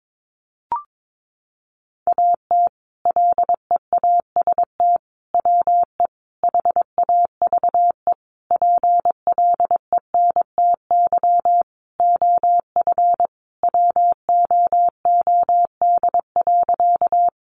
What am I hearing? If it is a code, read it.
Morse code, heard as a keyed tone, about 22 wpm, 700 Hz. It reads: AT LEAST WE HAVE PLENTY OF WOOD.
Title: Sentences from Top 1300 Words 22wpm